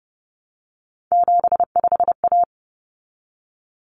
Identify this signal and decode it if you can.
Morse code, heard as a keyed tone, about 30 wpm, 700 Hz. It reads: 75A
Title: Common Cut Numbers - 3-Digit 30wpm